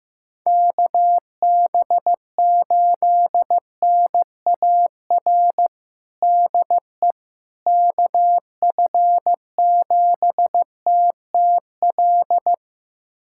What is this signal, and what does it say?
Morse code, heard as a keyed tone, about 15 wpm, 700 Hz. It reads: KB8NAR DE KF7TTL
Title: USA Callsign DE Callsign - 6 Characters - 15wpm